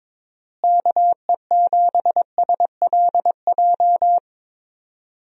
Morse code, heard as a keyed tone, about 22 wpm, 700 Hz. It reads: KE7SLJ
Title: USA Call Signs - 6 Characters - Repeated 3x - 22wpm